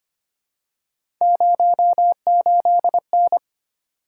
Morse code, heard as a keyed tone, about 25 wpm, 700 Hz. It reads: 08N
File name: Three Letter-Number Combinations - 25wpm